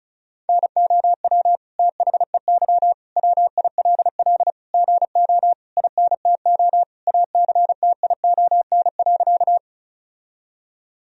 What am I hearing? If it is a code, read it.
Morse code, heard as a keyed tone, about 36 wpm, 700 Hz. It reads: NOW THEY WILL GO INTO ACTION.
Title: Sentences from Top 1000 Words - Review Part 4 - 35wpm